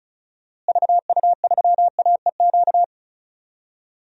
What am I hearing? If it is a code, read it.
Morse code, heard as a keyed tone, about 36 wpm, 700 Hz. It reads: VU3AEQ
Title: DXCC-100 Call Signs - 6 Characters - Repeated 3x - 35wpm